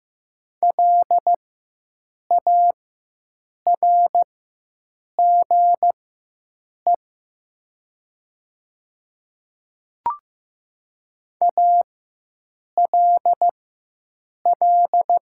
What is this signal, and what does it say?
Morse code, heard as a keyed tone, about 15 wpm, spaced as for 7 wpm, 690 Hz. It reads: LARGE ALL
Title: 4x Character Spacing - Top 500 Words - 15wpm